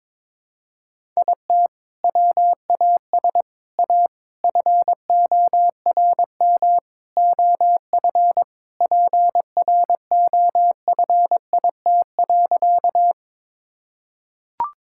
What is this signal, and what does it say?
Morse code, heard as a keyed tone, about 22 wpm, 700 Hz. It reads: IT WAS A FORM OF PROFIT.
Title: Sentences from Top 1000 Words - Review Part 5 - 22wpm